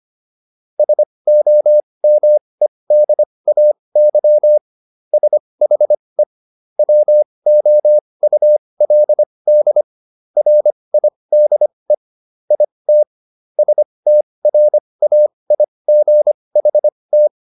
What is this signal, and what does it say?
Morse code, heard as a keyed tone, about 25 wpm, spaced as for 20 wpm, 600 Hz. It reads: SOMEDAY SHE WOULD RIDE IT STRAIGHT